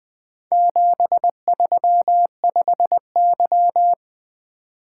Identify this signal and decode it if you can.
Morse code, heard as a keyed tone, about 20 wpm, 700 Hz. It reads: 735Y
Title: Four Letter-Number Combinations - 20wpm